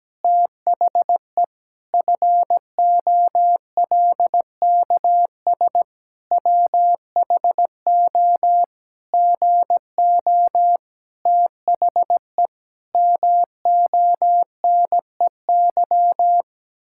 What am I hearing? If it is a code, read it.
Morse code, heard as a keyed tone, about 17 wpm, 700 Hz. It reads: THE FOLKS WHO GO THE MONEY
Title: Pangram Challenge v2 - 17wpm